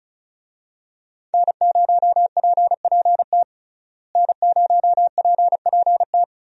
Morse code, heard as a keyed tone, about 36 wpm, 700 Hz. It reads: N0PPT N0PPT
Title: USA Call Signs - 5 Characters - Repeated 3x - 35wpm